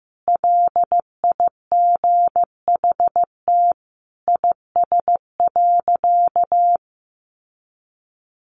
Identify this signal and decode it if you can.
Morse code, heard as a keyed tone, about 15 wpm, 700 Hz. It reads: LIGHT IS.
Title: Sentences from Top 1000 Words 15wpm